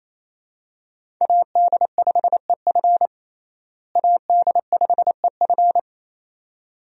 Morse code, heard as a keyed tone, about 28 wpm, 700 Hz. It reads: AD5EF AD5EF
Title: USA Call Signs - 5 Characters - Repeated 2x - 28wpm